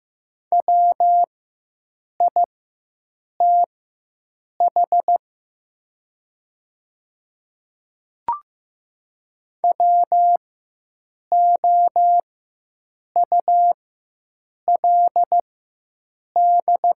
Morse code, heard as a keyed tone, about 15 wpm, spaced as for 7 wpm, 690 Hz. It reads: WITH WOULD
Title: 4x Character Spacing - Top 500 Words - 15wpm